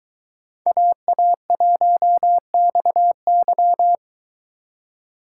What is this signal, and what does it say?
Morse code, heard as a keyed tone, about 22 wpm, 700 Hz. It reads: AA1XY